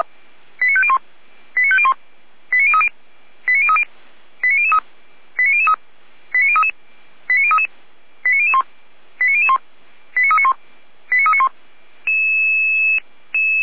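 Five-tone selective call (ZVEI), heard as quick runs of five tones, tones of about 70 ms. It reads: 89501 89501 89030 89030 89003 89003 89033 89033 89001 89001 89391 89391
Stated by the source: ZVEI five-tone selective call 89050 received by SDR.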